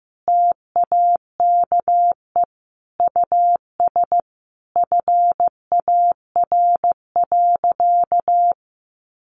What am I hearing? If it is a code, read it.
Morse code, heard as a keyed tone, about 15 wpm, 700 Hz. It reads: TAKE US FAR.
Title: Sentences from Top 1500 Words 15wpm